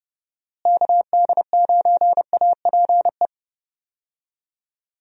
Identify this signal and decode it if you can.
Morse code, heard as a keyed tone, about 32 wpm, 700 Hz. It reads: KD9APE
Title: USA Call Signs - 6 Characters - Repeated 2x - 30wpm